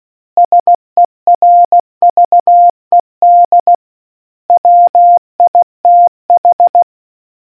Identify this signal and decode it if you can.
Morse code, heard as a keyed tone, about 16 wpm, 700 Hz. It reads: SERVED WITH